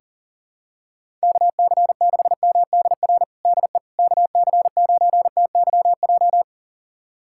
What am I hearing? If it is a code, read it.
Morse code, heard as a keyed tone, about 40 wpm, 700 Hz. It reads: KC6MDR DE KC9TYJ
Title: USA Callsign DE Callsign - 6 Characters - 40wpm